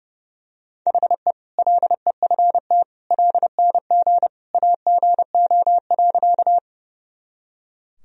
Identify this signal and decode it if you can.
Morse code, heard as a keyed tone, about 32 wpm, 700 Hz. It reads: HE LEFT LNG AGO.
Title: Sentences from Top 500 Words and Abbreviations 30wpm